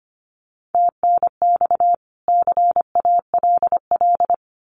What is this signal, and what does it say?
Morse code, heard as a keyed tone, about 25 wpm, 700 Hz. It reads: TNX CALL
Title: Common QSO Phrases 25wpm